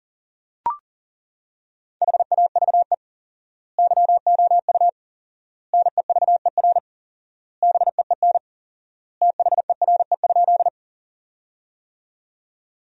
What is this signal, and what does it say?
Morse code, heard as a keyed tone, about 37 wpm, 700 Hz. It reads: HAVE YOU NEVER BEEN THERE?
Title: 4x Word-Spacing - Sentences from Top 300 Words - 40wpm